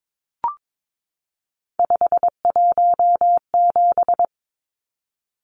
Morse code, heard as a keyed tone, about 22 wpm, 700 Hz. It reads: 517
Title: Numbers - 3-Digit 22wpm